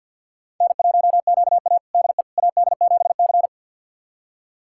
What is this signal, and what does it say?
Morse code, heard as a keyed tone, about 48 wpm, 690 Hz. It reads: N1QA DE AD7C
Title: USA Callsign DE Callsign - 4 Characters - 50wpm